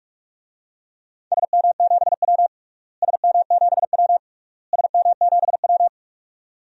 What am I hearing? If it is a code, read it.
Morse code, heard as a keyed tone, about 46 wpm, 700 Hz. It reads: SM7W SM7W SM7W
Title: DXCC-100 Call Signs - 4 Characters - Repeated 3x - 45wpm